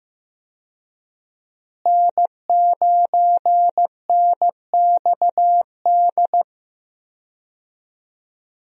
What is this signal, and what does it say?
Morse code, heard as a keyed tone, about 16 wpm, 700 Hz. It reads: N9NXD
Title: USA Call Signs - 5 Characters - Repeated 2x - 15wpm